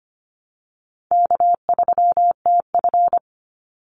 Morse code, heard as a keyed tone, about 25 wpm, 700 Hz. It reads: K3TF
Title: Easy Call Signs Repeated 3x - 25wpm